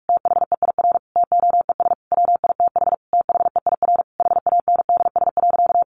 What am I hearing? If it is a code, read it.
Morse code, heard as a keyed tone, about 44 wpm, 700 Hz. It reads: THEIR TOES WITH THEIR HANDS.
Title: Easy Phrases 45wpm